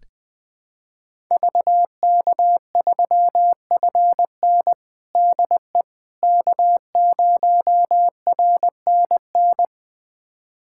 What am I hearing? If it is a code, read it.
Morse code, heard as a keyed tone, about 20 wpm, 700 Hz. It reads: VK3FN DE K0RNN
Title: USA Callsign DE Callsign - 5 Characters - 20wpm